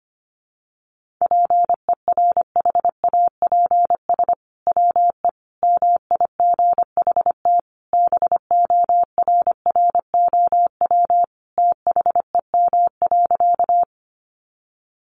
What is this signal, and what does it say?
Morse code, heard as a keyed tone, about 25 wpm, 700 Hz. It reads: PERHAPS WE MIGHT BORROW THEM.
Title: Sentences from Top 1500 Words 25wpm